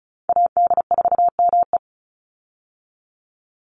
Morse code, heard as a keyed tone, about 34 wpm, 700 Hz. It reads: AD4ME
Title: USA Call Signs 35wpm